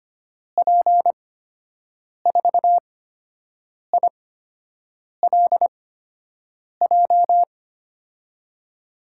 Morse code, heard as a keyed tone, about 25 wpm, spaced as for 7 wpm, 700 Hz. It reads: P4ILJ
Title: USA Call Signs - 8x Character Spacing - 25wpm